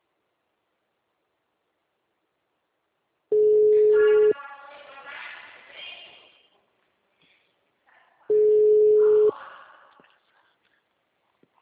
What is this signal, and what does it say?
Ringback tone (Europe, 425 Hz).